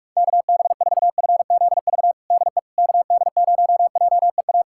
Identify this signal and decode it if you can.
Morse code, heard as a keyed tone, about 46 wpm, 690 Hz. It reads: KB4FZV DE KD0JEA
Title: USA Callsign DE Callsign - 6 Characters - 45wpm